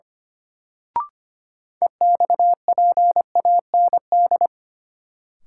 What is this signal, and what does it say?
Morse code, heard as a keyed tone, about 25 wpm, 700 Hz. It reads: EXPAND
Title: Sentences from Top 1400 Words - New Word Drill - 25wpm